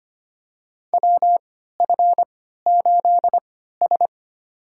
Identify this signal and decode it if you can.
Morse code, heard as a keyed tone, about 25 wpm, spaced as for 14 wpm, 700 Hz. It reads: WF8S
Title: USA Call Signs - 4 Characters - 3x Character Spacing - 25wpm